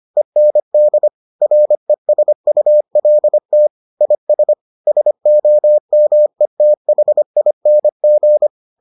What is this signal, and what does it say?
Morse code, heard as a keyed tone, about 25 wpm, 600 Hz. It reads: END RESULT IS SOMETHING